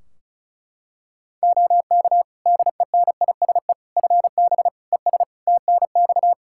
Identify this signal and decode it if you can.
Morse code, heard as a keyed tone, about 36 wpm, 700 Hz. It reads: OK DENISE FB ES TNX
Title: Practice QSO - Exchange 2 - Info - 35wpm